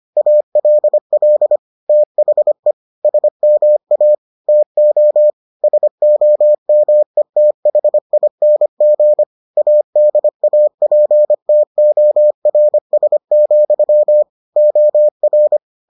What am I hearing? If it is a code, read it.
Morse code, heard as a keyed tone, about 25 wpm, 600 Hz. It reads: ALL THE SMA TO SOMETHING ADAPTORS, OR